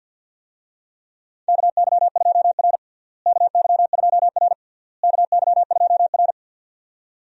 Morse code, heard as a keyed tone, about 50 wpm, 700 Hz. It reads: KY2R KY2R KY2R
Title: USA Call Signs - 4 Characters - Repeated 3x - 50wpm